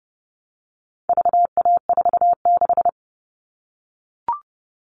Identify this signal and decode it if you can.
Morse code, heard as a keyed tone, about 30 wpm, 700 Hz. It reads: VA46
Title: All Cut Numbers - 4-Digit 30wpm